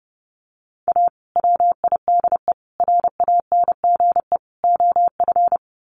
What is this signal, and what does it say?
Morse code, heard as a keyed tone, about 30 wpm, 700 Hz. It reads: A WIDE RANGE OF
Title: Sets of 4 Words 30wpm